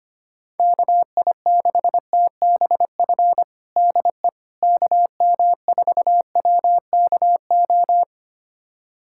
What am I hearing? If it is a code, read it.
Morse code, heard as a keyed tone, about 25 wpm, 700 Hz. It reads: KI6TBF DE KM4WKO
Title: USA Callsign DE Callsign 25wpm